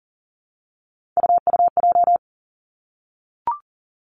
Morse code, heard as a keyed tone, about 40 wpm, 700 Hz. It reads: UUJ
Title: Three Challenging Letter-Number Combinations 40wpm